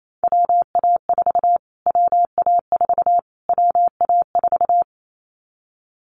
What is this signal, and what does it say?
Morse code, heard as a keyed tone, about 28 wpm, 700 Hz. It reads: WA4 WA4 WA4
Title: Callsign Prefixes for Top 25 DXCC - Repeated 3x - 28wpm